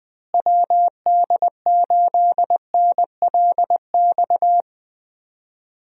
Morse code, heard as a keyed tone, about 20 wpm, 700 Hz. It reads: WD8NLX
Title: USA Call Signs - 6 Characters - Repeated 2x - 20wpm